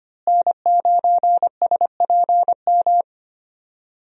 Morse code, heard as a keyed tone, about 25 wpm, 700 Hz. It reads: N9SPM
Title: USA Call Signs - 5 Characters - Repeated 2x - 25wpm